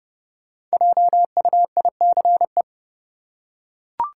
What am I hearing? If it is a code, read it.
Morse code, heard as a keyed tone, about 30 wpm, 700 Hz. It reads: JUICE
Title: Five Letter Words - 30wpm